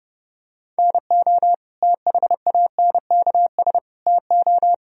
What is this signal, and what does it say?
Morse code, heard as a keyed tone, about 30 wpm, 700 Hz. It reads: NO THANKS TO
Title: Sentences from Top 1000 Words - Review Part 3 - 30wpm